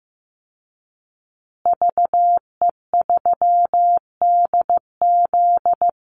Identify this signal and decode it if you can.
Morse code, heard as a keyed tone, about 15 wpm, 700 Hz. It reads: VE3DZ
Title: Call Signs Repeated 3x - 15wpm